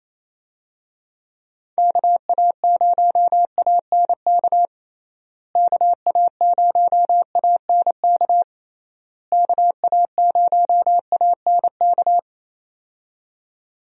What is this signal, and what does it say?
Morse code, heard as a keyed tone, about 27 wpm, 700 Hz. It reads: KA0ANK KA0ANK KA0ANK
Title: USA Call Signs - 6 Characters - Repeated 3x - 28wpm